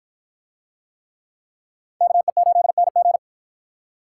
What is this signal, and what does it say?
Morse code, heard as a keyed tone, about 51 wpm, 700 Hz. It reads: KE8NG